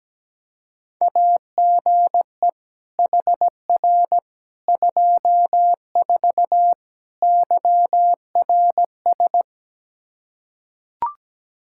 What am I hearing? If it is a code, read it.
Morse code, heard as a keyed tone, about 17 wpm, 700 Hz. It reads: AGE HR 24 YRS
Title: Practice QSO - Exchange 3 - More Info - 17wpm